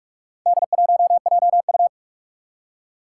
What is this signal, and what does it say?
Morse code, heard as a keyed tone, about 46 wpm, 700 Hz. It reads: D1JU
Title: Four Challenging Letter-Number Combinations 45wpm